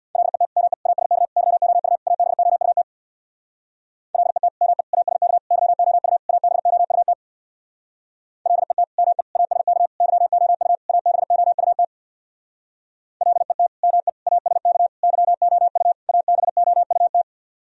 Morse code, heard as a keyed tone, about 62 wpm, 700 Hz. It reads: LET ME ASK YOU ABOUT LET ME ASK YOU ABOUT LET ME ASK YOU ABOUT LET ME ASK YOU ABOUT
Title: Speed-Racing - Sets of 5 Words 50wpm